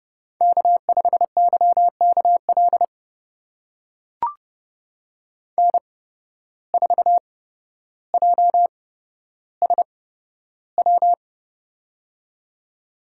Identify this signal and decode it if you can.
Morse code, heard as a keyed tone, about 30 wpm, spaced as for 13 wpm, 700 Hz. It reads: K5YKL N4JSW
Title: USA Call Signs - 5 Characters - 8x Character Spacing - 30wpm